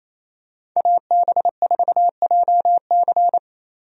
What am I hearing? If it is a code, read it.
Morse code, heard as a keyed tone, about 28 wpm, 700 Hz. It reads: AB4JC
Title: USA Call Signs - 5 Characters - 28wpm